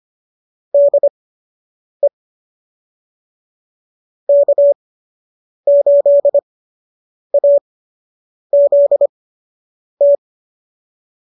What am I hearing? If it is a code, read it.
Morse code, heard as a keyed tone, about 25 wpm, spaced as for 8 wpm, 590 Hz. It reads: DE K8AZT
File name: qso006 08